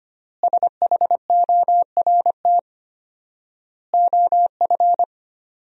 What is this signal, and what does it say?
Morse code, heard as a keyed tone, about 25 wpm, 700 Hz. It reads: SHORT OF
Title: 4x Word-Spacing - Sentences from Top 400 Words - 25wpm